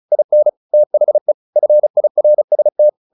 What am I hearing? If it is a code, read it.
Morse code, heard as a keyed tone, about 34 wpm, 600 Hz. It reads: IN THE FIRST
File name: the old man and the sea 2 35wpm